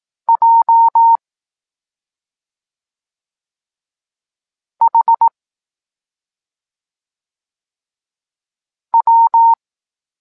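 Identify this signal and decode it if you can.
Morse code, heard as a keyed tone, about 18 wpm, 920 Hz. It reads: J H W
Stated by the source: Evenly spaced Morse lesson text 21 for learning letters G and Z, from SP9MOA Niepołomicki Klub Krótkofalowców.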